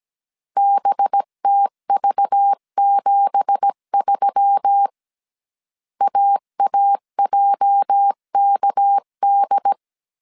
Morse code, heard as a keyed tone, about 17 wpm, 800 Hz. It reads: BTV73 AAJKD
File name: CW / DET / 1 9 1500C S B 16wpm 2856